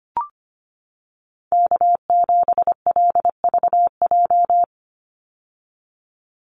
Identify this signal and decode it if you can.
Morse code, heard as a keyed tone, about 25 wpm, 700 Hz. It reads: K7LVJ
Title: Call Signs - 25wpm